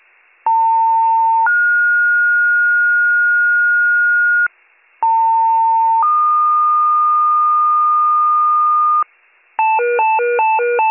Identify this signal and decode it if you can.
Two-tone page, heard as a one-second tone, then a three-second tone, about 900 then 1430 Hz, about 900 then 1220 Hz, with noise. It is heard from 0.5 to 9.0 s.